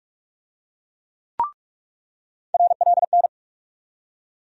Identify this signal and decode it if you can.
Morse code, heard as a keyed tone, about 46 wpm, 700 Hz. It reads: RLN
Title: USA Call Sign Postfixes - 3 Letters 45wpm